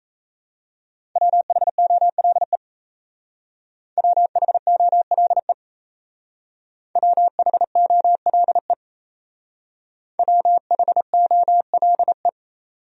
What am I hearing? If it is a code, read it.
Morse code, heard as a keyed tone, about 34 wpm, 700 Hz. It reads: WHOLE WHOLE WHOLE WHOLE
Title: Speed-Racing Top 500 Words 28wpm